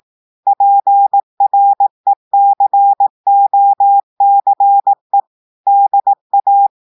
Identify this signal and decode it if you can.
Morse code, heard as a keyed tone, about 18 wpm, 800 Hz. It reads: PRECOCE DA
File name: audios / portugais / portugais texte en clair 1 vitesse 18 mots minutes